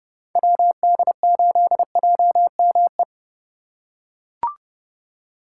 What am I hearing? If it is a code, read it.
Morse code, heard as a keyed tone, about 28 wpm, 700 Hz. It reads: WD8JME